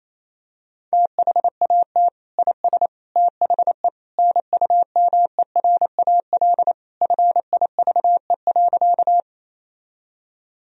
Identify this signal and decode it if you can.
Morse code, heard as a keyed tone, about 28 wpm, 700 Hz. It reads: THAT IS THE NUMERAL FIVE.
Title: Sentences from Top 1000 Words - Review Part 2 - 28wpm